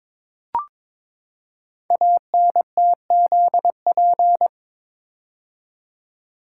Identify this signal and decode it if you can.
Morse code, heard as a keyed tone, about 22 wpm, 700 Hz. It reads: ANTZP
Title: Code Groups - 5 Letters - 22wpm